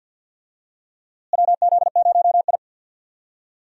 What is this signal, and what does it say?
Morse code, heard as a keyed tone, about 52 wpm, 700 Hz. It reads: WZ0I
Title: USA Call Signs - 4 Characters - Repeated 3x - 50wpm